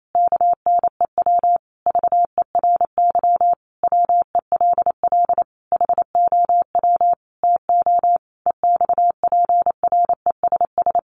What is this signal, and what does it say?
Morse code, heard as a keyed tone, about 28 wpm, 700 Hz. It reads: KNEW VERY WELL HOW TO EXPRESS
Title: Sentences from Top 1100 Words 28wpm